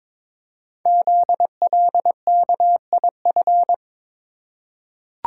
Morse code, heard as a keyed tone, about 23 wpm, 700 Hz. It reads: ZLKIF
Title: Code Groups - 5 Letters - 22wpm